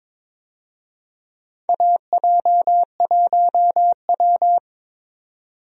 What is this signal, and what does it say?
Morse code, heard as a keyed tone, about 22 wpm, 700 Hz. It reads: AJ1W
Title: USA Call Signs - 4 Characters - 22wpm